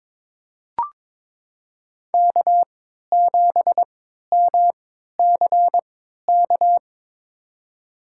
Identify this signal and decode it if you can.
Morse code, heard as a keyed tone, about 22 wpm, spaced as for 12 wpm, 700 Hz. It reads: K7MCK
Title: USA Call Signs - 3x Character Spacing - 22wpm